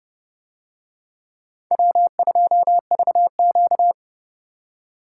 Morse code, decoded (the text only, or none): W2VQ